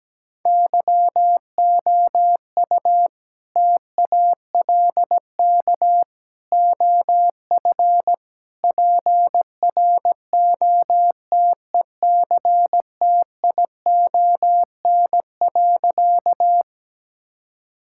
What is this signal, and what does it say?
Morse code, heard as a keyed tone, about 17 wpm, 700 Hz. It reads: YOU TALK OF PROTECTION.
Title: Sentences from Top 1600 Words 17wpm